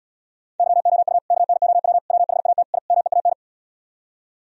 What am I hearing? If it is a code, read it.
Morse code, heard as a keyed tone, about 77 wpm, 700 Hz. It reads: YOU KNOW WHAT I MEAN